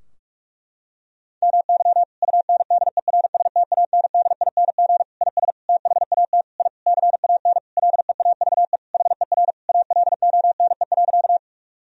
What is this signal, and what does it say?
Morse code, heard as a keyed tone, about 42 wpm, 700 Hz. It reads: MY UNDERSTANDING IS THAT I CAN LEAVE HER ALONE.
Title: Sentences from Top 1000 Words - Review Part 3 - 45wpm